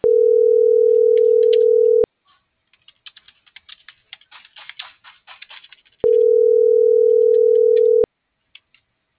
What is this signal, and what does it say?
Ringback tone (North America), 440 + 480 Hz.